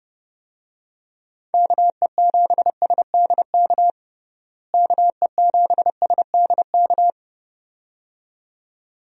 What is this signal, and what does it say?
Morse code, heard as a keyed tone, about 30 wpm, 700 Hz. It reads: KE7SDK KE7SDK
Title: USA Call Signs - 6 Characters - Repeated 2x - 30wpm